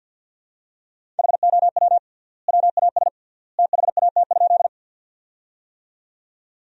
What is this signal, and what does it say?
Morse code, heard as a keyed tone, about 50 wpm, 700 Hz. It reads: HOW WAS THAT?
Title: 3x Word-Spacing - Sentences from Top 100 Words - 50wpm